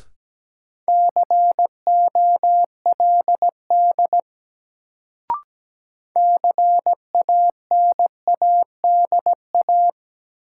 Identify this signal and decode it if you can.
Morse code, heard as a keyed tone, about 17 wpm, 700 Hz. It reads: COLD CANADA
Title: Sets of 5 Words - New Word Drill - 17wpm